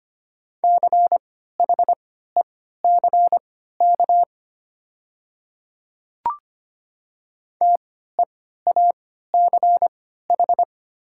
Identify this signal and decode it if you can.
Morse code, heard as a keyed tone, about 25 wpm, spaced as for 14 wpm, 700 Hz. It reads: CHECK TEACH
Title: 3x Character Spacing - Top 500 Words - 25wpm